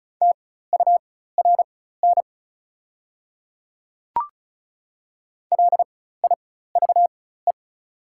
Morse code, heard as a keyed tone, about 34 wpm, spaced as for 16 wpm, 700 Hz. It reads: TURN LIVE
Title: 4x Character Spacing - Top 200 Words - 35wpm